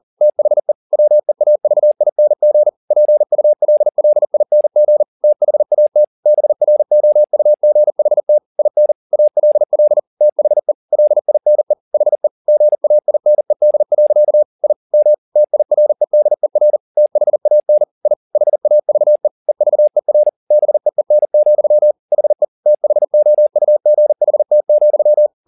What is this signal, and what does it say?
Morse code, heard as a keyed tone, about 40 wpm, 600 Hz. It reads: THE WEAVING PULLING THAT BROUGHT IN ALL THE LINE HE GAINED. I M TIREDER THAN I HAVE EVER BEEN, HE THOUGHT,
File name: the old man and the sea 2 40wpm